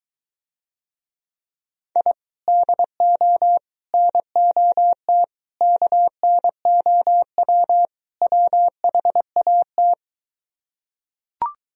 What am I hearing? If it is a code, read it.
Morse code, heard as a keyed tone, about 23 wpm, 700 Hz. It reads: I DO NOT KNOW WHAT